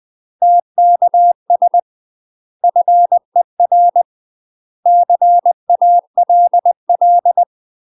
Morse code, heard as a keyed tone, about 20 wpm, 700 Hz. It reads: TKS FER CALL
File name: UK MASTER QSOs TRAINING/20WPM